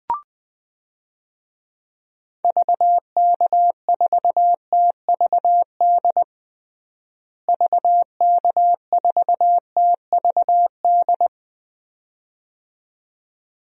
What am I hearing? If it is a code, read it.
Morse code, heard as a keyed tone, about 21 wpm, 700 Hz. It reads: VK4TVD VK4TVD
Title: DXCC-100 Call Signs - 6 Characters - Repeated 2x - 20wpm